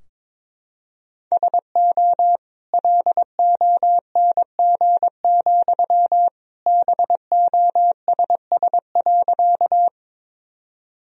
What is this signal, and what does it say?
Morse code, heard as a keyed tone, about 22 wpm, 700 Hz. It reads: SO LONG, BOSS.
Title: Sentences from Top 1000 Words - Review Part 4 - 22wpm